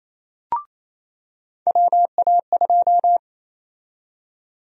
Morse code, heard as a keyed tone, about 28 wpm, 700 Hz. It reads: WA2
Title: USA Call Sign Prefixes - 28wpm